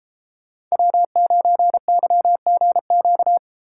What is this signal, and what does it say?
Morse code, heard as a keyed tone, about 31 wpm, 700 Hz. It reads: W9YGQ